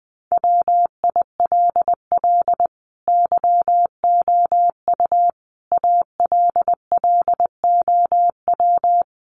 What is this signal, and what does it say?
Morse code, heard as a keyed tone, about 19 wpm, 700 Hz. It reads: WILL YOU ALLOW
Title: Sentences from Top 700 Words 20wpm